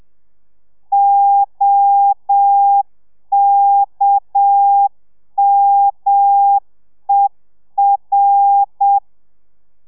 Morse code, heard as a keyed tone, about 7 wpm, 800 Hz. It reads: OKMER